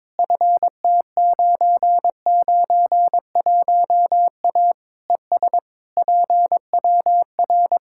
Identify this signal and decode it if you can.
Morse code, heard as a keyed tone, about 22 wpm, 700 Hz. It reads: FT991A ES PWR
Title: Practice QSO - Exchange 2 - Info - 22wpm